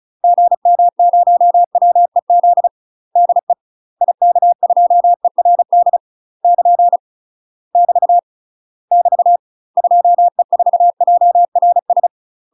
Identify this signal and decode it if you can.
Morse code, heard as a keyed tone, about 36 wpm, 700 Hz. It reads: GM0WEZ DE IK2ERD <KN> <BT> <BT> 2E4JRS
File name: UK MASTER QSOs TRAINING / 35WPM / UK MASTER QSOs 0027